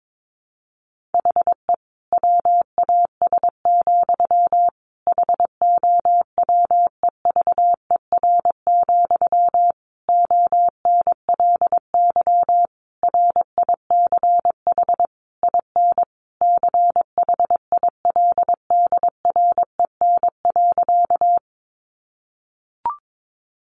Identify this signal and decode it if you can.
Morse code, heard as a keyed tone, about 22 wpm, 700 Hz. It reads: HE WAS, HOWEVER, ONLY RICH IN CHILDREN.
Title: Sentences from Top 1100 Words 22wpm